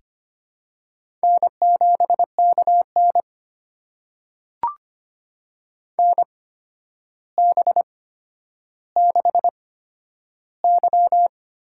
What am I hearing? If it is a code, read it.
Morse code, heard as a keyed tone, about 25 wpm, spaced as for 11 wpm, 700 Hz. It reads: N7KN NB6Y
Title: USA Call Signs - 4 Characters - 8x Character Spacing - 25wpm